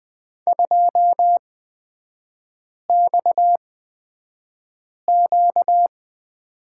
Morse code, heard as a keyed tone, about 20 wpm, spaced as for 5 wpm, 700 Hz. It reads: 2XQ
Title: 236 Lesson 38e - Letter q - Three Characters 20wpm